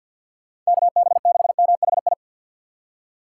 Morse code, heard as a keyed tone, about 65 wpm, spaced as for 57 wpm, 700 Hz. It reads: KB6MHI
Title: Speed Racing - USA 2x3 Call Signs 50wpm